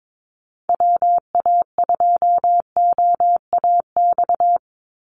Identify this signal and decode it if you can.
Morse code, heard as a keyed tone, about 22 wpm, 700 Hz. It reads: WA2OAX
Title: Call Signs Repeated 2x - 22wpm